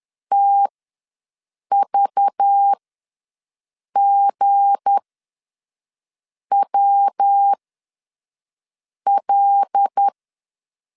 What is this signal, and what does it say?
Morse code, heard as a keyed tone, about 10 wpm, spaced as for 5 wpm, 800 Hz. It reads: TVGWL
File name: LEZIONI / 8 YJ45 / 8 YJ ws 10wpm 4043